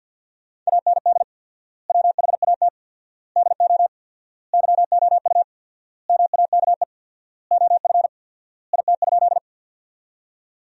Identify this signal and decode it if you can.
Morse code, heard as a keyed tone, about 50 wpm, 700 Hz. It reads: AND WHAT DO YOU MAKE OF IT?
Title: 4x Word-Spacing - Sentences from Top 100 Words - 50wpm